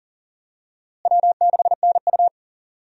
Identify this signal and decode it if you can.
Morse code, heard as a keyed tone, about 39 wpm, 700 Hz. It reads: W6NU
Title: USA Call Signs - 4 Characters - Repeated 3x - 40wpm